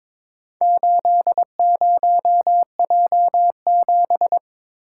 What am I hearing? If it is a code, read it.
Morse code, heard as a keyed tone, about 22 wpm, 700 Hz. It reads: 80J7
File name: practice / 22wpm / Four Letter-Number Combinations - 22wpm